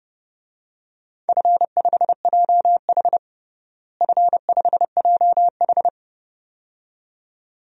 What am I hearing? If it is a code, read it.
Morse code, heard as a keyed tone, about 31 wpm, 700 Hz. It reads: F5JH F5JH
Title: DXCC-100 Call Signs - 4 Characters - Repeated 2x - 30wpm